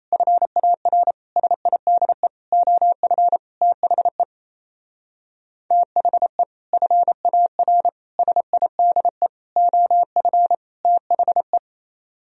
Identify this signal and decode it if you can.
Morse code, heard as a keyed tone, about 34 wpm, 700 Hz. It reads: FAR SIDE OF THE THE FAR SIDE OF THE
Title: Speed-Racing - Sets of 5 Words 28wpm